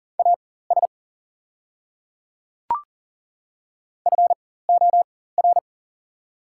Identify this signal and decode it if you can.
Morse code, heard as a keyed tone, about 40 wpm, spaced as for 19 wpm, 700 Hz. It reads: AS FOR
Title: 4x Character Spacing - Top 100 Words - 40wpm